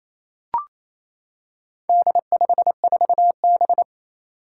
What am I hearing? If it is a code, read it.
Morse code, heard as a keyed tone, about 28 wpm, 700 Hz. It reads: D54B
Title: Four Challenging Letter-Number Combinations 28wpm